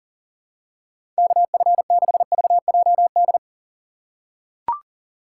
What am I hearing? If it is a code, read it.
Morse code, heard as a keyed tone, about 40 wpm, 700 Hz. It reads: KF6VJD
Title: USA Call Signs - 6 Characters - 2x Character Spacing - 40wpm